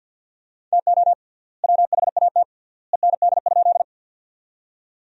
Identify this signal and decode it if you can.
Morse code, heard as a keyed tone, about 50 wpm, 700 Hz. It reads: TO WHAT END?
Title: 3x Word-Spacing - Sentences from Top 200 Words - 50wpm